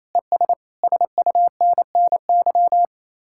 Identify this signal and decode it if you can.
Morse code, heard as a keyed tone, about 35 wpm, 700 Hz. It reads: ES SUNNY
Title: Speed-Racing Common QSO Phrases - 28wpm